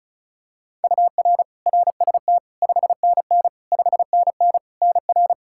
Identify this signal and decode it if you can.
Morse code, heard as a keyed tone, about 36 wpm, 700 Hz. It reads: UR RST 5NN 5NN NR
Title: K1USN SST QSO Practice 35wpm